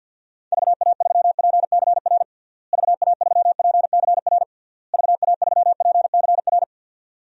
Morse code, heard as a keyed tone, about 51 wpm, 700 Hz. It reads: VA3PCR VA3PCR VA3PCR
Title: DXCC-100 Call Signs - 6 Characters - Repeated 3x - 50wpm